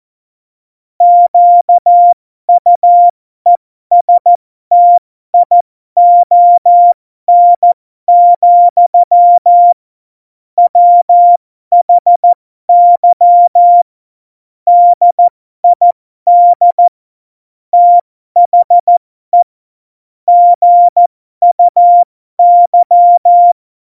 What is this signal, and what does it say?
Morse code, heard as a keyed tone, about 14 wpm, spaced as for 12 wpm, 700 Hz. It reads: QUESTION, WHY DID THE GUY
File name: Morsekurs / Witze